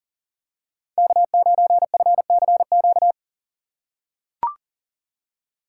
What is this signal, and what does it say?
Morse code, heard as a keyed tone, about 40 wpm, 700 Hz. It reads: K9FCQ
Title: USA Call Signs - 5 Characters - 4x Character Spacing - 40wpm